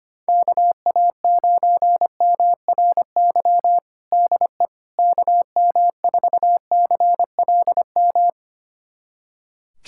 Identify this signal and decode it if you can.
Morse code, heard as a keyed tone, about 25 wpm, 700 Hz. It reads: KA9MRY DE KM4CLM
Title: USA Callsign DE Callsign - 6 Characters - 25wpm